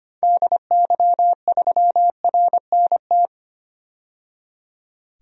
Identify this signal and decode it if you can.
Morse code, heard as a keyed tone, about 25 wpm, 700 Hz. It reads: DY3RNT
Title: DXCC-100 Call Signs - 6 Characters - 25wpm